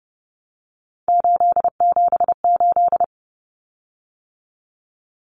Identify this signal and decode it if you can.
Morse code, heard as a keyed tone, about 30 wpm, 700 Hz. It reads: 878
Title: Common Cut Numbers - 3-Digit 30wpm